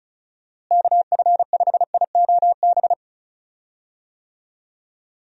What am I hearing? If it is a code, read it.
Morse code, heard as a keyed tone, about 34 wpm, 700 Hz. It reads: KF5IOB
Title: USA Call Signs - 6 Characters - Repeated 2x - 35wpm